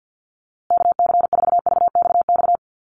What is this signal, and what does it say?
Morse code, heard as a keyed tone, about 50 wpm, 700 Hz. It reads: KC4VXX
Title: USA Call Signs Repeated 2x 50wpm